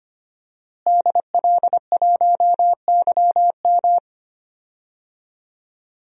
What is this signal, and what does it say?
Morse code, heard as a keyed tone, about 25 wpm, 700 Hz. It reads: DL1YM
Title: DXCC-100 Call Signs - 5 Characters - Repeated 2x - 25wpm